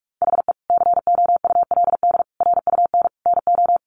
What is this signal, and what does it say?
Morse code, heard as a keyed tone, about 44 wpm, 700 Hz. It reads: HE COULD RUN NO
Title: Sentences from Top 300 Words 45wpm